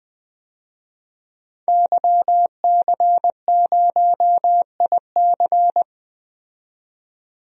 Morse code, heard as a keyed tone, about 20 wpm, 700 Hz. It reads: YC0IC
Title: DXCC-100 Call Signs - 5 Characters - 20wpm